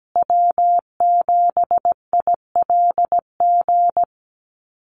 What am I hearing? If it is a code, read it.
Morse code, heard as a keyed tone, about 17 wpm, 700 Hz. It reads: W7ILG
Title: USA Call Signs Repeated 3x 17wpm